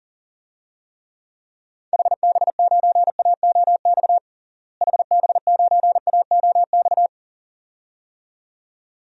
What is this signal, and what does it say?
Morse code, heard as a keyed tone, about 37 wpm, 700 Hz. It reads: HB9AOX HB9AOX
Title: DXCC-100 Call Signs - 6 Characters - Repeated 2x - 40wpm